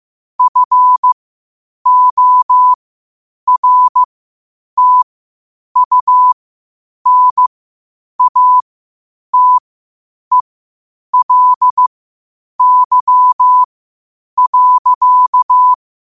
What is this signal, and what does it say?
Morse code, heard as a keyed tone, about 15 wpm, spaced as for 8 wpm, 1000 Hz. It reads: FORTUNATELY.
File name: qotdfiles / 8wpm